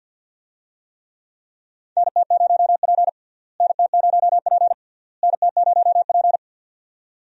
Morse code, heard as a keyed tone, about 50 wpm, 700 Hz. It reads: NT0P NT0P NT0P
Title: USA Call Signs - 4 Characters - Repeated 3x - 50wpm